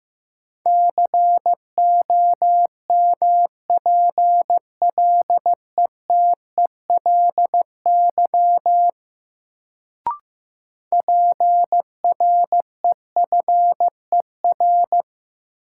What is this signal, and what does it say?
Morse code, heard as a keyed tone, about 15 wpm, 700 Hz. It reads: COMPLETELY PREFER
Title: Sentences from Top 1100 Words - New Word Drill Alt - 15wpm